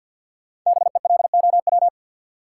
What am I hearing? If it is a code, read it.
Morse code, heard as a keyed tone, about 52 wpm, 700 Hz. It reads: BELOW
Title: Sentences from Top 1000 Words - Review Part 5 - New Word Drill - 50wpm